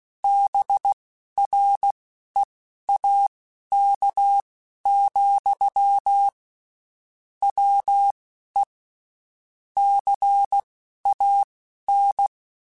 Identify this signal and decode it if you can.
Morse code, heard as a keyed tone, about 16 wpm, spaced as for 11 wpm, 800 Hz. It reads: BREAK, WE CAN
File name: (spaced) / QSOs Spaced-mp3s / WA7QOX 1 @15 sp